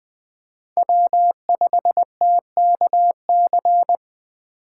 Morse code, heard as a keyed tone, about 20 wpm, 700 Hz. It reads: W5TKC
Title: USA Call Signs - 5 Characters - 3x Character Spacing - 20wpm